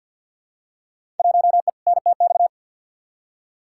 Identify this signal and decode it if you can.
Morse code, heard as a keyed tone, about 52 wpm, 700 Hz. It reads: 1E NTX
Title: Field Day Exchanges - 50wpm